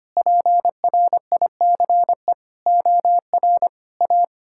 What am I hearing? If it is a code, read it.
Morse code, heard as a keyed tone, about 25 wpm, 700 Hz. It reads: PRICE OR A
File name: practice / 25wpm / Sentences with 9-Word Phrases - 25wpm